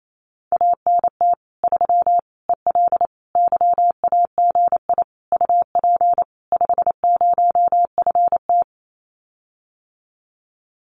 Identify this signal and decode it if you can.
Morse code, heard as a keyed tone, about 28 wpm, 700 Hz. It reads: ANT 3 EL YAGI UP 50FT
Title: Common QSO Phrases Encore 28wpm